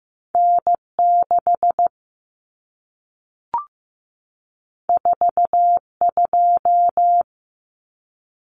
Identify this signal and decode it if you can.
Morse code, heard as a keyed tone, about 15 wpm, 700 Hz. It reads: N 6 4 2
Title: Common Cut Numbers - 2-Digit 15wpm